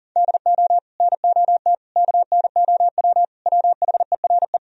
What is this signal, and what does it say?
Morse code, heard as a keyed tone, about 52 wpm, 700 Hz. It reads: DO NOT KNOW WHERE